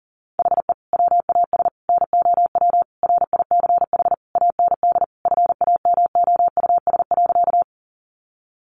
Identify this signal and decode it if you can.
Morse code, heard as a keyed tone, about 40 wpm, 700 Hz. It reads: HE WAS NOW RICH AND FAMOUS.
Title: Sentences from Top 1500 Words 40wpm